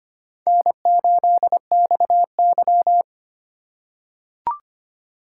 Morse code, heard as a keyed tone, about 25 wpm, 700 Hz. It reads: N8XY
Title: USA Call Signs - 4 Characters - 6x Character Spacing - 25wpm